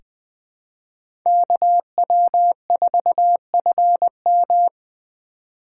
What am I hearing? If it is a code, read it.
Morse code, heard as a keyed tone, about 20 wpm, 700 Hz. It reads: KW4FM
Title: 2023 POTA Hunter Call Signs - 20wpm